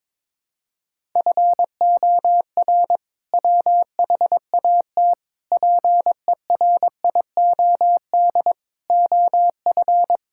Morse code, heard as a keyed tone, about 22 wpm, 700 Hz. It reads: FOR WHAT PERIOD OF
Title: Sentences from Top 1000 Words - Review Part 5 - 22wpm